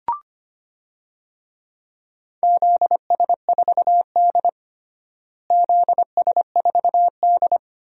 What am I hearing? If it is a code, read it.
Morse code, heard as a keyed tone, about 25 wpm, 700 Hz. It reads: ZS4D ZS4D
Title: DXCC-100 Call Signs - 4 Characters - Repeated 3x - 25wpm